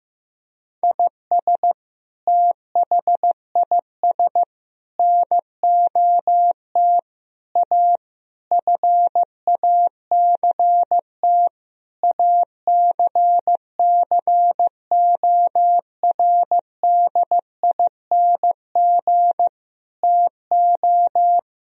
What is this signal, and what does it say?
Morse code, heard as a keyed tone, about 15 wpm, 700 Hz. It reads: IS THIS NOT A FACT ACCORDING TO
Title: Sentences from Top 1000 Words - Review Part 4 - 15wpm